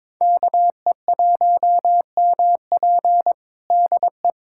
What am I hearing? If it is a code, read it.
Morse code, heard as a keyed tone, about 22 wpm, 700 Hz. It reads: KE1MP DE
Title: USA Callsign DE Callsign - 5 Characters - 22wpm